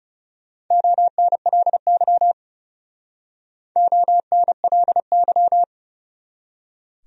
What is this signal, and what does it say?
Morse code, heard as a keyed tone, about 35 wpm, 700 Hz. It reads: ONLY ONLY
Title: Speed-Racing Top 300 Words 30wpm